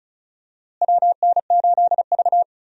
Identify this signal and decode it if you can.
Morse code, heard as a keyed tone, about 34 wpm, 700 Hz. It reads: WN8V
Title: USA Call Signs - 4 Characters - Repeated 2x - 35wpm